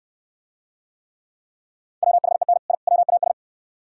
Morse code, heard as a keyed tone, about 69 wpm, 700 Hz. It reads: WHEN I WAS